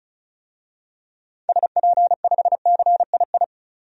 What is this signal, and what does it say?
Morse code, heard as a keyed tone, about 36 wpm, 700 Hz. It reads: SP5CII
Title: DXCC-100 Call Signs - 6 Characters - 35wpm